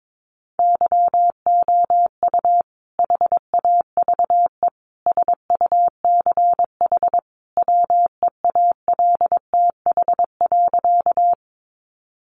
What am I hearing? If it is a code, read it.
Morse code, heard as a keyed tone, about 22 wpm, 700 Hz. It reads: YOU HAVE SUCH WEALTH.
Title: Sentences from Top 1200 Words 22wpm